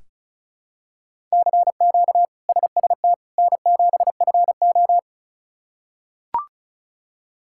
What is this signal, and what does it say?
Morse code, heard as a keyed tone, about 36 wpm, 700 Hz. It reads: CQ SST N7FO
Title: K1USN SST QSO Practice - 35wpm